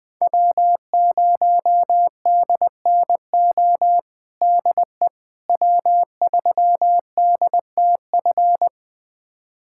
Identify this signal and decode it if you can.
Morse code, heard as a keyed tone, about 21 wpm, 700 Hz. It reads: W0DNO DE W3DTF
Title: USA Callsign DE Callsign - 5 Characters - 20wpm